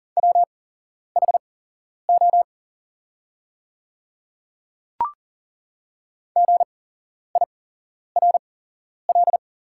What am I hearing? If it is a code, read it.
Morse code, heard as a keyed tone, about 40 wpm, spaced as for 11 wpm, 690 Hz. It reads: WHO GIRL
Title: 8x Character Spacing - Top 500 Words - 40wpm